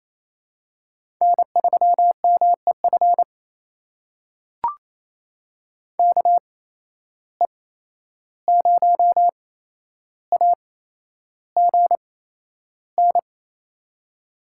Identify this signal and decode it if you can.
Morse code, heard as a keyed tone, about 28 wpm, spaced as for 11 wpm, 700 Hz. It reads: N3MEF KE0AGN
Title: USA Call Signs - 8x Character Spacing - 28wpm